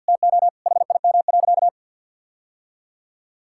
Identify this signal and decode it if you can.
Morse code, heard as a keyed tone, about 52 wpm, 700 Hz. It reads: TO HIM.
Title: Sentences from Top 1000 Words - Review Part 6 - 50wpm